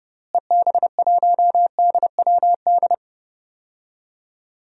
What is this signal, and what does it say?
Morse code, heard as a keyed tone, about 30 wpm, 700 Hz. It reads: EB1DWD
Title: DXCC-100 Call Signs - 6 Characters - Repeated 3x - 30wpm